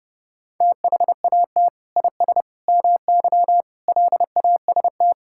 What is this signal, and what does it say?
Morse code, heard as a keyed tone, about 30 wpm, 700 Hz. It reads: THAT IS MY LAST
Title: Sentences from Top 1600 Words 30wpm